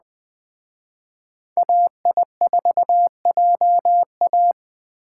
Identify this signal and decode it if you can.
Morse code, heard as a keyed tone, about 24 wpm, 700 Hz. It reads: AI4JA
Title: Speed Racing - USA 2x2 Call Signs 20wpm